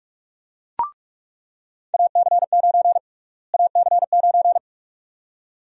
Morse code, heard as a keyed tone, about 46 wpm, 700 Hz. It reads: AC9 AC9
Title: USA Call Sign Prefixes - Repeated 2x - 45wpm